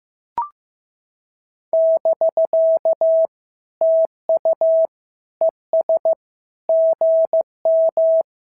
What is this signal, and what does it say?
Morse code, heard as a keyed tone, about 14 wpm, 650 Hz. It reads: <BK> TU ES GM
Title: SOTA QSO Practice 15wpm